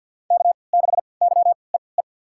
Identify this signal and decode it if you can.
Morse code, heard as a keyed tone, about 51 wpm, spaced as for 28 wpm, 700 Hz. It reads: K6YEE
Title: USA Call Signs - 5 Characters - 3x Character Spacing - 50wpm